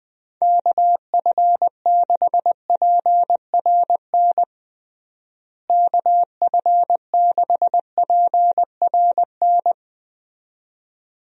Morse code, heard as a keyed tone, about 21 wpm, 700 Hz. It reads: KF6PRN KF6PRN
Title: USA Call Signs - 6 Characters - Repeated 2x - 20wpm